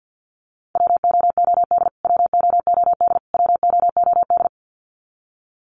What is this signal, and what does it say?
Morse code, heard as a keyed tone, about 49 wpm, 700 Hz. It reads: WOOD WOOD WOOD
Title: Top 300 Words Repeated 50wpm